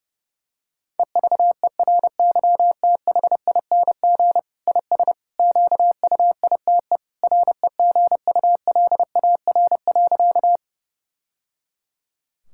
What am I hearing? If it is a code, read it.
Morse code, heard as a keyed tone, about 30 wpm, 700 Hz. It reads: EVERYTHING IS QUITE REGULAR.
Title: Sentences from Top 1000 Words - Review Part 6 - 30wpm